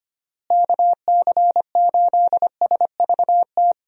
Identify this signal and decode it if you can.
Morse code, heard as a keyed tone, about 25 wpm, 700 Hz. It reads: KC8SVT
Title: USA Call Signs - 6 Characters - Repeated 3x - 25wpm